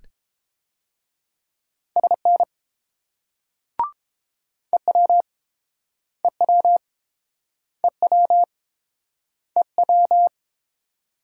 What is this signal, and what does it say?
Morse code, heard as a keyed tone, about 31 wpm, 700 Hz. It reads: SN EW EW EW EW